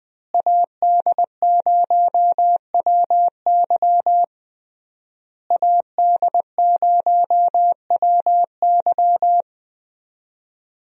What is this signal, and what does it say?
Morse code, heard as a keyed tone, about 20 wpm, 700 Hz. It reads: AD0WY AD0WY
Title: USA Call Signs - 5 Characters - Repeated 3x - 20wpm